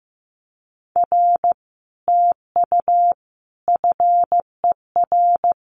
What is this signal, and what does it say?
Morse code, heard as a keyed tone, about 15 wpm, 700 Hz. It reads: R TU FER
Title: Common QSO Phrases Encore 15wpm